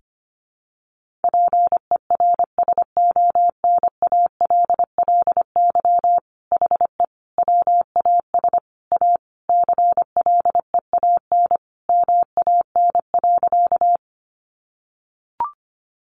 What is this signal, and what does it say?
Morse code, heard as a keyed tone, about 25 wpm, 700 Hz. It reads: PERSONALLY HE WAS A CLEAN MAN.
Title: Sentences from Top 1400 Words 25wpm